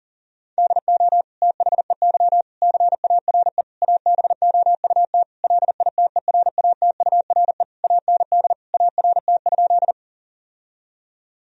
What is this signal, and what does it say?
Morse code, heard as a keyed tone, about 40 wpm, 700 Hz. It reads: DO THEY CARE ABOUT LITERATURE AND ART?
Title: Sentences from Top 1000 Words - Review Part 3 - 40wpm